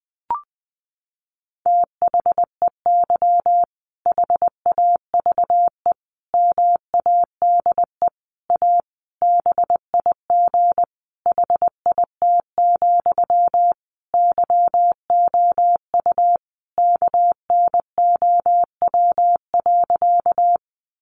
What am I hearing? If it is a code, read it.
Morse code, heard as a keyed tone, about 20 wpm, 700 Hz. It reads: THEY HAVE MADE A BIG HIT, YOU KNOW.
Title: Sentences from Top 1200 Words 20wpm